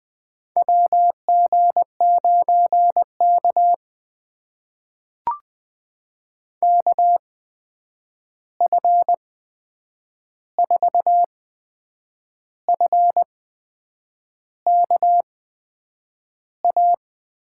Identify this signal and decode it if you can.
Morse code, heard as a keyed tone, about 20 wpm, spaced as for 8 wpm, 700 Hz. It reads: WG9K KF4F KA
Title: USA Call Signs - 8x Character Spacing - 20wpm